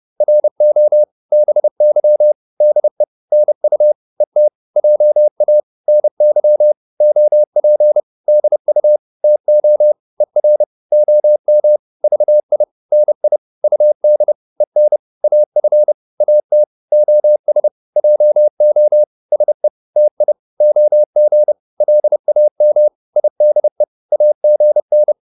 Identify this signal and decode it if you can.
Morse code, heard as a keyed tone, about 30 wpm, 600 Hz. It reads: RO BY DE NU ET JA NY OP DU TO ER OM VI NI UD EN AF AT OS JO SE TI OG LAM IDE AGN
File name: Korte ord DK 30wpm